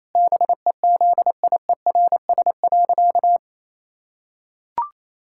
Morse code, heard as a keyed tone, about 28 wpm, 700 Hz. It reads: BEZIERS.
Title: Pangram Challenge v2 - 28wpm